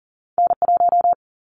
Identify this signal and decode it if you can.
Morse code, heard as a keyed tone, about 40 wpm, 700 Hz. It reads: N1
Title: Callsign Prefixes for Top 25 DXCC - 40wpm